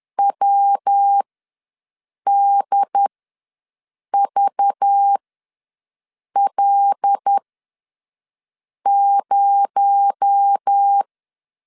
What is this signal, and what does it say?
Morse code, heard as a keyed tone, about 10 wpm, spaced as for 5 wpm, 800 Hz. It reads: WDVL0
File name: LEZIONI / 7 GF9Z / 7 G ws 10wpm 3843